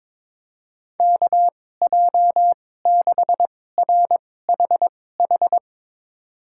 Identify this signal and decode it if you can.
Morse code, heard as a keyed tone, about 22 wpm, spaced as for 16 wpm, 700 Hz. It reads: KJ6RHH
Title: USA Call Signs - 6 Characters - 2x Character Spacing - 22wpm